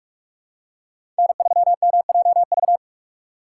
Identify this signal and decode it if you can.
Morse code, heard as a keyed tone, about 43 wpm, 700 Hz. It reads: N3MJV
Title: USA Call Signs - 5 Characters - 4x Character Spacing - 45wpm